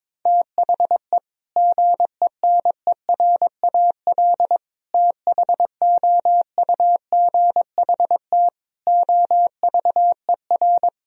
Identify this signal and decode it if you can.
Morse code, heard as a keyed tone, about 21 wpm, 700 Hz. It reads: THE GENERAL THOUGHT OVER
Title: Sentences from Top 1000 Words - Review Part 3 - 22wpm